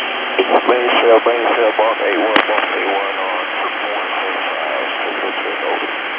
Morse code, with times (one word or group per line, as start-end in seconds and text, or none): none